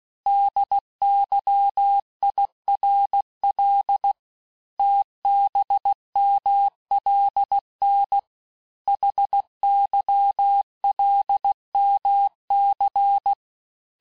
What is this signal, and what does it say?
Morse code, heard as a keyed tone, about 16 wpm, 800 Hz. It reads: DYIRL TBMLN HYLMC
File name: cours-du-lundi files / Cours-1586-texte-chiffre-vitesse-15mpm